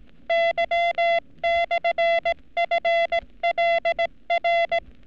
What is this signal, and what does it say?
Morse code, heard as a keyed tone, about 18 wpm, 680 Hz. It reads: Y/FLR
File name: training / NAVPERS 11224-11235 / NAVPERS 11233 RA RB